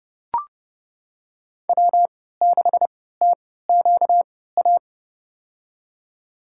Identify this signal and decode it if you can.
Morse code, heard as a keyed tone, about 30 wpm, spaced as for 17 wpm, 700 Hz. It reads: W6TQA
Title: USA Call Signs - 5 Characters - 3x Character Spacing - 30wpm